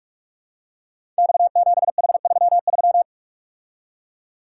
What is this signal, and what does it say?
Morse code, heard as a keyed tone, about 46 wpm, 700 Hz. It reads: X7H33
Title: Code Groups - 5 Letter:Numbers - 45wpm